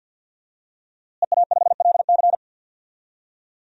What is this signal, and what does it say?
Morse code, heard as a keyed tone, about 49 wpm, 700 Hz. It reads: EA5LC
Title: DXCC-100 Call Signs - 5 Characters - Repeated 2x - 50wpm